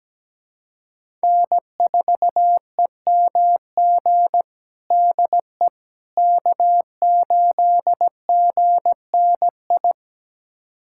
Morse code, heard as a keyed tone, about 17 wpm, 700 Hz. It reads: N4EMG DE K8GNI
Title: USA Callsign DE Callsign - 5 Characters - 17wpm